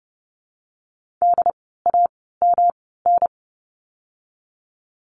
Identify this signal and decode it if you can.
Morse code, heard as a keyed tone, about 30 wpm, spaced as for 17 wpm, 700 Hz. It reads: DAMN
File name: course / 30:10wpm / 108 Lesson 18f - Letter m - Words 30wpm